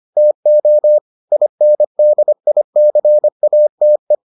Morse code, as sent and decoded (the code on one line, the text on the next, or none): - --- / .. -. -.. .. -.-. .- - .
TO INDICATE